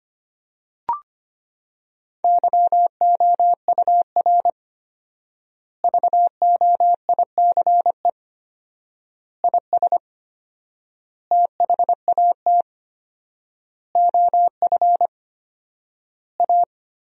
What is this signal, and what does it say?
Morse code, heard as a keyed tone, about 25 wpm, 700 Hz. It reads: YOUR VOICE IS THAT OF A
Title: 4x Word-Spacing - Sentences from Top 500 Words - 25wpm